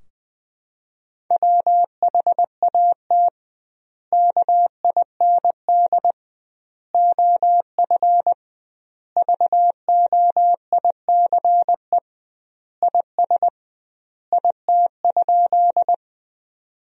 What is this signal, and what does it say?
Morse code, heard as a keyed tone, about 21 wpm, 700 Hz. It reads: WHAT KIND OF VOICE IS IT?
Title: 2x Word-Spacing - Sentences from Top 500 Words - 20wpm